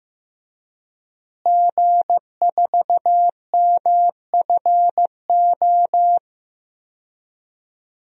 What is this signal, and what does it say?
Morse code, heard as a keyed tone, about 15 wpm, 700 Hz. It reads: G4MFO
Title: DXCC-100 Call Signs - 5 Characters - Repeated 2x - 15wpm